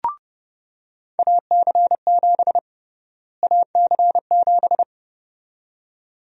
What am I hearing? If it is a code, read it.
Morse code, heard as a keyed tone, about 31 wpm, 700 Hz. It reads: AC7 AC7
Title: USA Call Sign Prefixes - Repeated 2x - 30wpm